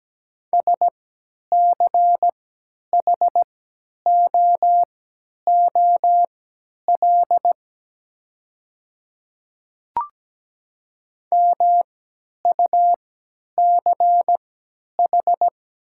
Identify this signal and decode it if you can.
Morse code, heard as a keyed tone, about 17 wpm, spaced as for 10 wpm, 700 Hz. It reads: SCHOOL MUCH
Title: 3x Character Spacing - Top 500 Words - 17wpm